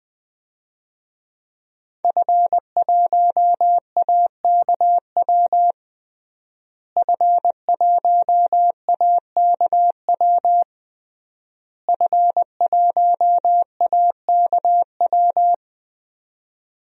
Morse code, heard as a keyed tone, about 20 wpm, 700 Hz. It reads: F1AKW F1AKW F1AKW
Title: DXCC-100 Call Signs - 5 Characters - Repeated 3x - 20wpm